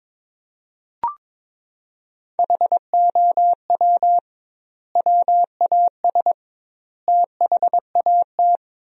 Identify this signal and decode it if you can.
Morse code, heard as a keyed tone, about 22 wpm, 700 Hz. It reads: HOW WAS THAT
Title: 2x Word-Spacing - Sentences from Top 500 Words - 22wpm